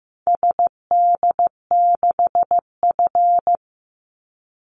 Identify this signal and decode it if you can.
Morse code, heard as a keyed tone, about 15 wpm, 700 Hz. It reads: SD6F
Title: Call Signs Repeated 3x - 15wpm